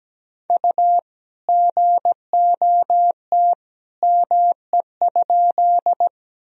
Morse code, heard as a keyed tone, about 17 wpm, 700 Hz. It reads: U GOT ME?
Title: Sentences from Top 300 Words and Abbreviations 17wpm